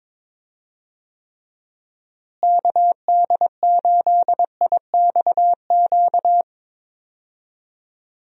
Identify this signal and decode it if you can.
Morse code, heard as a keyed tone, about 22 wpm, 700 Hz. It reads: KD8IXQ
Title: USA Call Signs - 6 Characters - Repeated 2x - 22wpm